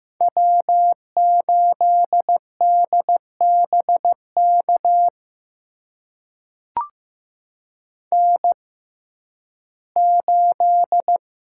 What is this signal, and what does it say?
Morse code, heard as a keyed tone, about 15 wpm, spaced as for 11 wpm, 700 Hz. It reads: W8DBK N8
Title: USA Call Signs - 6x Character Spacing - 15wpm